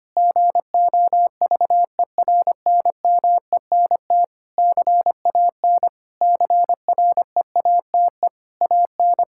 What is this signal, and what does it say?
Morse code, heard as a keyed tone, about 25 wpm, 700 Hz. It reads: GOVERNMENT CAN CREATE AN
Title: Sentences from Top 1600 Words 25wpm